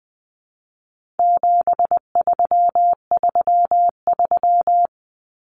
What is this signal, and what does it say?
Morse code, heard as a keyed tone, about 20 wpm, 700 Hz. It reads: 7333
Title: Numbers - 4-Digit 20wpm